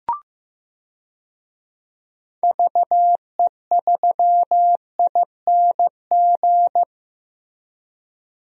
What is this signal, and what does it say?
Morse code, heard as a keyed tone, about 15 wpm, 700 Hz. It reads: VE3ING
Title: DXCC-100 Call Signs - 6 Characters - Repeated 3x - 15wpm